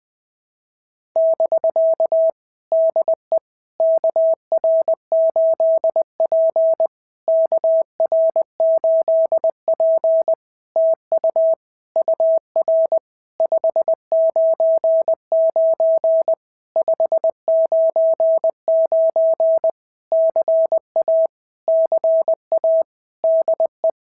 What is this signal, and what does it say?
Morse code, heard as a keyed tone, about 20 wpm, 650 Hz. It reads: <BK> DE KR8P KR8P TU UR 599 599 CA CA DE
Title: POTA QSO Practice - 20wpm